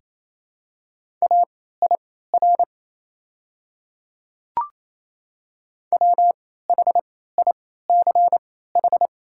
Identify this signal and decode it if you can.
Morse code, heard as a keyed tone, about 28 wpm, spaced as for 16 wpm, 700 Hz. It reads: AIR WHICH
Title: 3x Character Spacing - Top 200 Words - 28wpm